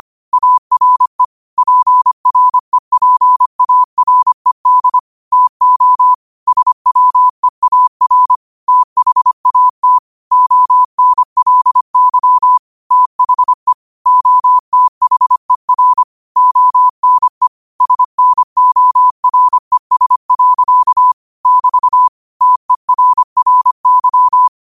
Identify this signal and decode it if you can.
Morse code, heard as a keyed tone, about 25 wpm, 1000 Hz. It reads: ARE PREPARED TO SWEAR THAT ONLY THE OTHER ONE SNORES. <BT> TERRY
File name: qotdfiles / 25wpm